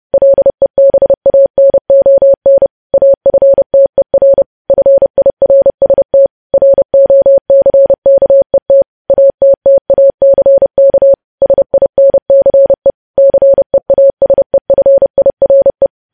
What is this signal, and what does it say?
Morse code, heard as a keyed tone, about 30 wpm, 550 Hz. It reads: LEBANON AFTER FIRST ROCKET ATTACK SINCE CEASEFIRE
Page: news headlines in morse code at 30 wpm